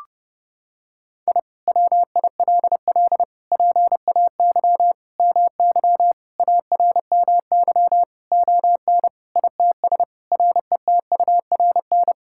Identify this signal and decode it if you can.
Morse code, heard as a keyed tone, about 32 wpm, 700 Hz. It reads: I WILL PAY MY ARMY ON ITS RETURN
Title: Sentences from Top 1000 Words - Review Part 5 - 30wpm